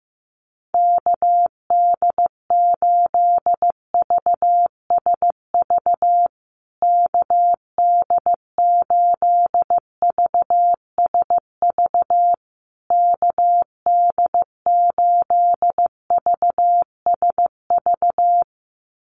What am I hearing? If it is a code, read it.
Morse code, heard as a keyed tone, about 14 wpm, 700 Hz. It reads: KD8VSV KD8VSV KD8VSV
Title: USA Call Signs Repeated 3x 15wpm